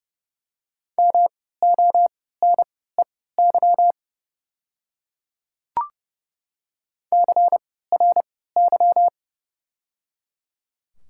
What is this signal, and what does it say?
Morse code, heard as a keyed tone, about 30 wpm, spaced as for 17 wpm, 700 Hz. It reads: MONEY CRY
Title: 3x Character Spacing - Top 500 Words - 30wpm